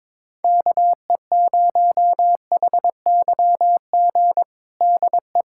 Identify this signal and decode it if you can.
Morse code, heard as a keyed tone, about 21 wpm, 700 Hz. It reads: KE0HYG DE
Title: USA Callsign DE Callsign - 6 Characters - 22wpm